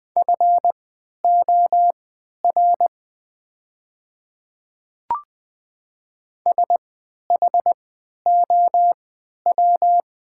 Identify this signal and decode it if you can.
Morse code, heard as a keyed tone, about 21 wpm, spaced as for 12 wpm, 700 Hz. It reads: FOR SHOW
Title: 3x Character Spacing - Top 300 Words - 20wpm